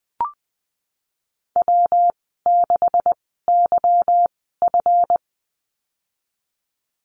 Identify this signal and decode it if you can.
Morse code, heard as a keyed tone, about 20 wpm, spaced as for 15 wpm, 700 Hz. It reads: W6YF
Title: 179 Lesson 29g - Slash - Call Signs 20wpm